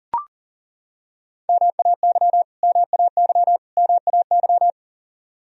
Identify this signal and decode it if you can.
Morse code, heard as a keyed tone, about 40 wpm, 700 Hz. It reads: MAY MAY MAY
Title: Top 100 Words Repeated 40wpm